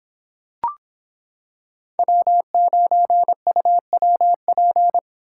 Morse code, heard as a keyed tone, about 26 wpm, 700 Hz. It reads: W9UWP